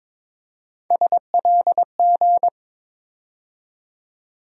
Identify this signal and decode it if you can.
Morse code, heard as a keyed tone, about 22 wpm, 700 Hz. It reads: SLG
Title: USA Call Sign Postfixes - 3 Letters 22wpm